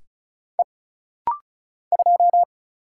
Morse code, heard as a keyed tone, about 34 wpm, 700 Hz. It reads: E 2
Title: Single Letter-Number Alt - Rapid-Fire 35wpm